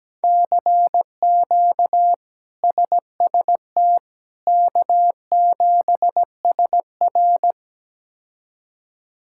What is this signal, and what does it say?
Morse code, heard as a keyed tone, about 17 wpm, 700 Hz. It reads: CQ SST K7SR
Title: K1USN SST QSO Practice - 17wpm